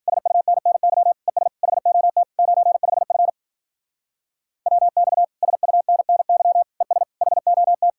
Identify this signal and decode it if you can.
Morse code, heard as a keyed tone, about 48 wpm, 700 Hz. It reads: SUNNY ES HOT 95F WX SUNNY ES HOT
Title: Speed-Racing Common QSO Phrases - 40wpm